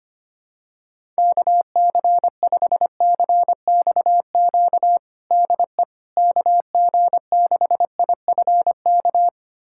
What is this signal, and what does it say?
Morse code, heard as a keyed tone, about 25 wpm, 700 Hz. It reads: KC5CXQ DE KG6IFK
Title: USA Callsign DE Callsign 25wpm